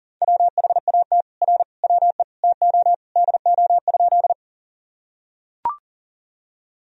Morse code, heard as a keyed tone, about 40 wpm, 700 Hz. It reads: WHAT R WE TO DO?
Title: Sentences from Top 100 Words and Abbreviations 40wpm